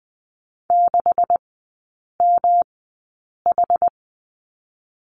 Morse code, heard as a keyed tone, about 20 wpm, spaced as for 8 wpm, 700 Hz. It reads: 6MH
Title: 138 Lesson 23e - Letter f - Three Characters 20wpm